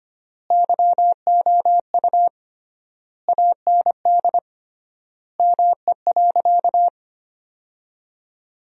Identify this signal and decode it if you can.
Morse code, heard as a keyed tone, about 25 wpm, 700 Hz. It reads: YOU AND ME.
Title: 3x Word-Spacing - Sentences from Top 500 Words - 25wpm